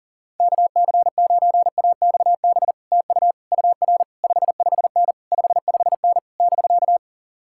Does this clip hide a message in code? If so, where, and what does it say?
Yes: Morse code from 0.4 to 7.0 s, heard as a keyed tone, about 52 wpm, 700 Hz. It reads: KC9AXB TU UR 55N 55N <BK>